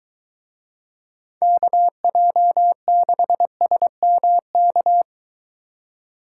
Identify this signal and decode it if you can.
Morse code, heard as a keyed tone, about 23 wpm, 700 Hz. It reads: KJ6SMK